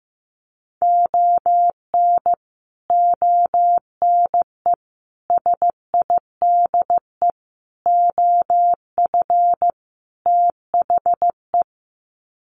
Morse code, heard as a keyed tone, about 15 wpm, 700 Hz. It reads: ON ONE SIDE OF THE
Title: Sets of 5 Words 15wpm